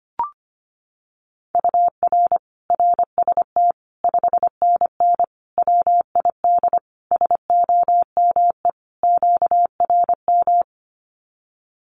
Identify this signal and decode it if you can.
Morse code, heard as a keyed tone, about 25 wpm, 700 Hz. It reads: UR RST 5NN WID SOME QRM
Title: Common QSO Phrases 25wpm